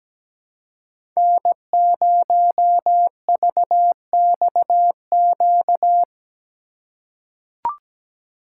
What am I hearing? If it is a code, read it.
Morse code, heard as a keyed tone, about 17 wpm, 700 Hz. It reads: N0VXQ
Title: USA Call Signs - 5 Characters - 3x Character Spacing - 17wpm